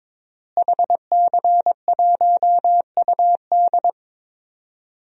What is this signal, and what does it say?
Morse code, heard as a keyed tone, about 22 wpm, 700 Hz. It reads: HC1UD
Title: DXCC-100 Call Signs - 5 Characters - Repeated 3x - 22wpm